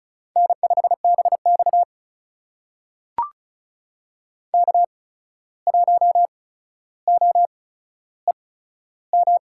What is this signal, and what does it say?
Morse code, heard as a keyed tone, about 36 wpm, spaced as for 14 wpm, 700 Hz. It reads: N5BX K1OEM
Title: USA Call Signs - 8x Character Spacing - 35wpm